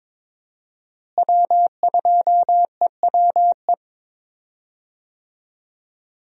Morse code, heard as a keyed tone, about 22 wpm, 700 Hz. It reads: W2EWE
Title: USA Call Signs - 5 Characters - Repeated 3x - 22wpm